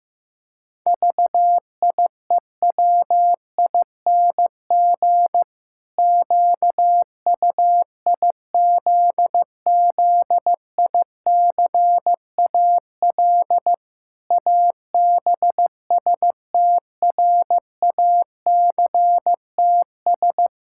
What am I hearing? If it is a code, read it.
Morse code, heard as a keyed tone, about 16 wpm, 700 Hz. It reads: VIEWING QUIZZICAL ABSTRACTS
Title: Pangram Challenge v2 - 15wpm